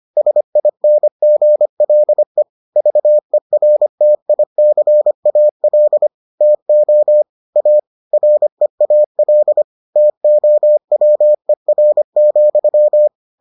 Morse code, heard as a keyed tone, about 25 wpm, 600 Hz. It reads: SINGLE VERTICAL TO A REAL TOWER,